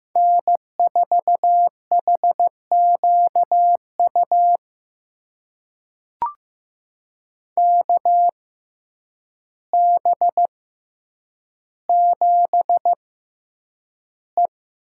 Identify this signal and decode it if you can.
Morse code, heard as a keyed tone, about 15 wpm, spaced as for 8 wpm, 700 Hz. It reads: N4HQU KB7E
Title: USA Call Signs - 6x Character Spacing - 15wpm